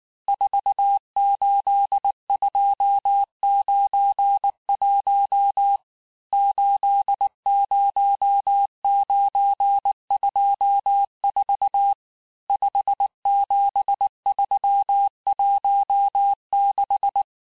Morse code, heard as a keyed tone, about 19 wpm, 800 Hz. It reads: 48291 80924 57316
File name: cours-du-lundi files / cours-1583--texte-chiffre-18-mpm